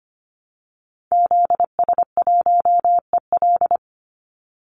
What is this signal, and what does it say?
Morse code, heard as a keyed tone, about 25 wpm, 700 Hz. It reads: ZS1EL
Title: Call Signs Repeated 2x - 25wpm